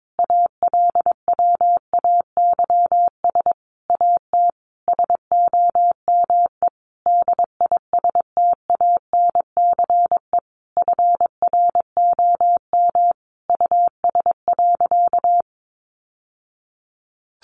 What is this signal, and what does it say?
Morse code, heard as a keyed tone, about 22 wpm, 700 Hz. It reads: ALWAYS AT SOME DISTANCE FROM US.
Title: Sentences from Top 1200 Words 22wpm